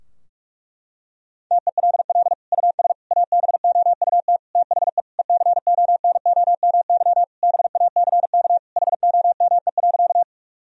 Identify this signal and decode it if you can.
Morse code, heard as a keyed tone, about 46 wpm, 700 Hz. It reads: TELL US ABOUT THE ECONOMY BACK HOME.
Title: Sentences from Top 1000 Words - Review Part 3 - 45wpm